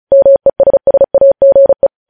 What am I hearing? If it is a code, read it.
Morse code, heard as a keyed tone, about 36 wpm, 550 Hz. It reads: MESSAGE
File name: podcast / News headlines-1x-35x35